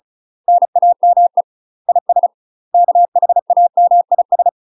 Morse code, heard as a keyed tone, about 36 wpm, 700 Hz. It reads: NAME IS KHAMIS
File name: UK MASTER QSOs TRAINING / 35WPM / UK MASTER QSOs 0014